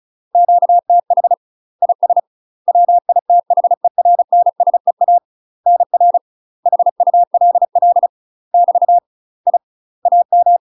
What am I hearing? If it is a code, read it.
Morse code, heard as a keyed tone, about 35 wpm, 700 Hz. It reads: QTH IS WITHERNSEA NR HULL <BT> I AM
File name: UK MASTER QSOs TRAINING / 35WPM / UK MASTER QSOs 0032